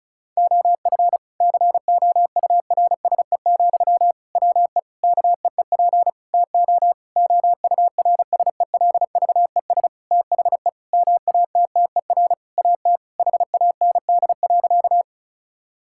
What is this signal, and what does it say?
Morse code, heard as a keyed tone, about 36 wpm, 700 Hz. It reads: OF COURSE, WE KEEP TO OURSELVES THE MATTER AT HAND.
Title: Sentences from Top 1600 Words 35wpm